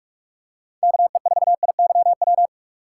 Morse code, heard as a keyed tone, about 46 wpm, 700 Hz. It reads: KE4IYW
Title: USA Call Signs - 6 Characters - 3x Character Spacing - 45wpm